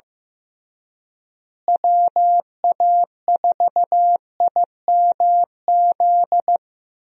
Morse code, heard as a keyed tone, about 16 wpm, 700 Hz. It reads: WA4IMZ
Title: USA Call Signs - 6 Characters - Repeated 3x - 15wpm